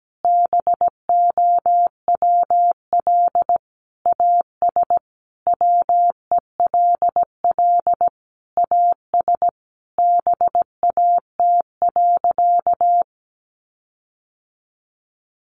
Morse code, heard as a keyed tone, about 17 wpm, 700 Hz. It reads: BOWL AS WELL AS BAT.
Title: Sentences from Top 1500 Words 17wpm